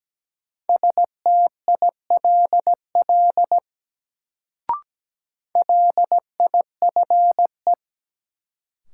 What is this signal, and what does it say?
Morse code, heard as a keyed tone, about 17 wpm, 700 Hz. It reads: STILL LIFE
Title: Sentences from Top 300 Words - New Word Drill - 17wpm